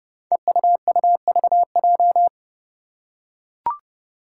Morse code, heard as a keyed tone, about 31 wpm, 700 Hz. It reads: EUUVJ
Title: Five Challenging Letter-Number Combinations 30wpm